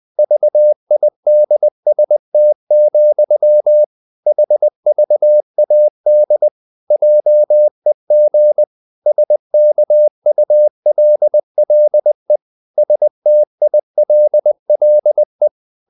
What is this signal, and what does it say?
Morse code, heard as a keyed tone, about 20 wpm, 600 Hz. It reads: VIDST, HVAD JEG SKULLE STILLE
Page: Ferie 20 wpm